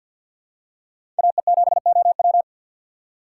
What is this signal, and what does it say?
Morse code, heard as a keyed tone, about 49 wpm, 700 Hz. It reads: AE7OW
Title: USA Call Signs - 4x Character Spacing - 50wpm